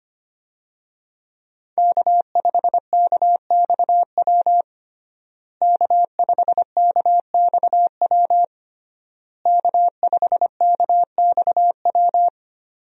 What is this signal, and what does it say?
Morse code, heard as a keyed tone, about 25 wpm, 700 Hz. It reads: K5KXW K5KXW K5KXW
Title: USA Call Signs - 5 Characters - Repeated 3x - 25wpm